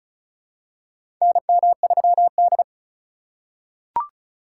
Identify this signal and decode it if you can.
Morse code, heard as a keyed tone, about 34 wpm, 700 Hz. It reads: NM3D
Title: USA Call Signs - 4 Characters - 4x Character Spacing - 35wpm